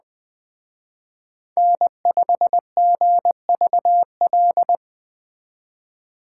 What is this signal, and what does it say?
Morse code, heard as a keyed tone, about 20 wpm, 700 Hz. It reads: N5GVL